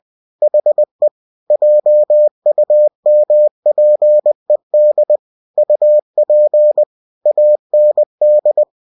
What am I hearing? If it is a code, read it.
Morse code, heard as a keyed tone, about 20 wpm, 600 Hz. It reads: HE JUMPED UP AND